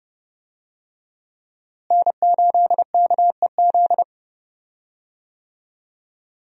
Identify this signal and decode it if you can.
Morse code, heard as a keyed tone, about 32 wpm, 700 Hz. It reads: N8KEZ
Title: USA Call Signs - 5 Characters - Repeated 2x - 30wpm